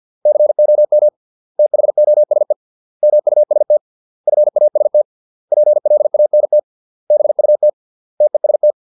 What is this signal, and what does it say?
Morse code, heard as a keyed tone, about 50 wpm, 600 Hz. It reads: COM THOSE MUST FAST PLANT BUT TEST